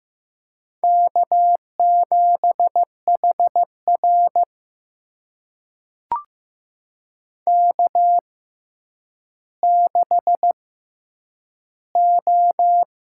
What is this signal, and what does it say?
Morse code, heard as a keyed tone, about 15 wpm, spaced as for 9 wpm, 700 Hz. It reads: K7HR K6O
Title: USA Call Signs - 4 Characters - 6x Character Spacing - 15wpm